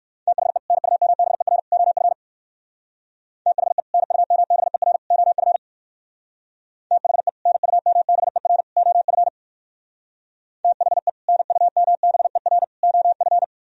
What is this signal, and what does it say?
Morse code, heard as a keyed tone, about 58 wpm, 700 Hz. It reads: THE NUMBER OF THE NUMBER OF THE NUMBER OF THE NUMBER OF